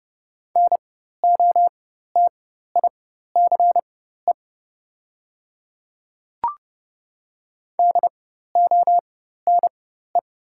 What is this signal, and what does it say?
Morse code, heard as a keyed tone, about 30 wpm, spaced as for 14 wpm, 690 Hz. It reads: NOTICE DONE
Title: 4x Character Spacing - Top 500 Words - 30wpm